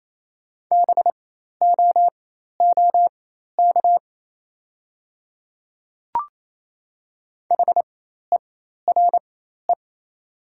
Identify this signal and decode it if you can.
Morse code, heard as a keyed tone, about 28 wpm, spaced as for 13 wpm, 700 Hz. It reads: BOOK HERE
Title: 4x Character Spacing - Top 300 Words - 28wpm